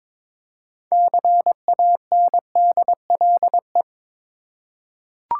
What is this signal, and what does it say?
Morse code, heard as a keyed tone, about 22 wpm, 700 Hz. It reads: CANDLE
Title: Six Letter Words - 22wpm